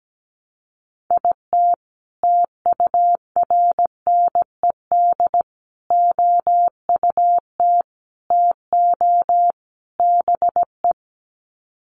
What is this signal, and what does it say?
Morse code, heard as a keyed tone, about 17 wpm, 700 Hz. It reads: IT TURNED OUT TO BE
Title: Sets of 5 Words 17wpm